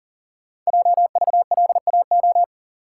Morse code, heard as a keyed tone, about 41 wpm, 700 Hz. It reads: JVLAO
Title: Code Groups - 5 Letter:Numbers - 40wpm